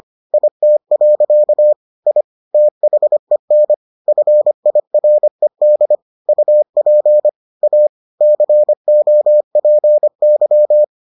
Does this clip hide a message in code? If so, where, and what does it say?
Yes: Morse code from 0.3 to 10.8 s, heard as a keyed tone, about 25 wpm, 600 Hz. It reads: IT. I THEN FIRED UP A COPY